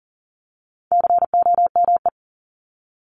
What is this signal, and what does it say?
Morse code, heard as a keyed tone, about 40 wpm, 700 Hz. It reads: COME
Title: Top 300 Words 40wpm